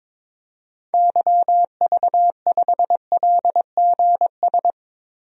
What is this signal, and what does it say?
Morse code, heard as a keyed tone, about 21 wpm, 700 Hz. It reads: YV5LGS
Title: DXCC-100 Call Signs - 6 Characters - 22wpm